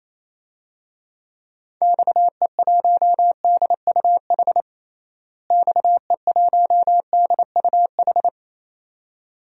Morse code, heard as a keyed tone, about 28 wpm, 700 Hz. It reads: XE1DUH XE1DUH
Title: DXCC-100 Call Signs - 6 Characters - Repeated 2x - 28wpm